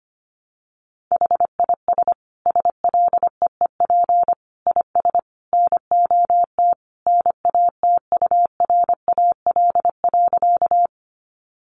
Morse code, heard as a keyed tone, about 25 wpm, 700 Hz. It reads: HIS SLEEP IS NOT NATURAL.
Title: Sentences from Top 700 Words 25wpm